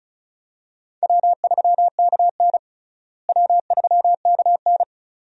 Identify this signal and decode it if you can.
Morse code, heard as a keyed tone, about 34 wpm, 700 Hz. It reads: W3KN W3KN
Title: USA Call Signs - 4 Characters - Repeated 2x - 35wpm